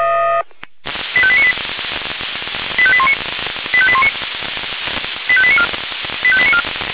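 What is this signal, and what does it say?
Five-tone selective call (ZVEI), heard as quick runs of five tones, tones of about 70 ms. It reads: 85898 85819 85819 85894 85894
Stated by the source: ZVEI five-tone selective call 85898 received on SDR.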